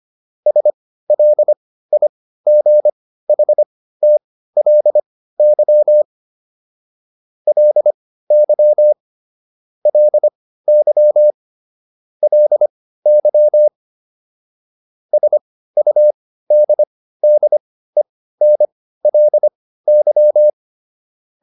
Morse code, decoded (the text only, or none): SLIGHTLY LY LY LY SUDDENLY